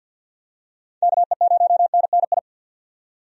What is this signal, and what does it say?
Morse code, heard as a keyed tone, about 51 wpm, 700 Hz. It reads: KE0NNI
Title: USA Call Signs - 6 Characters - 50wpm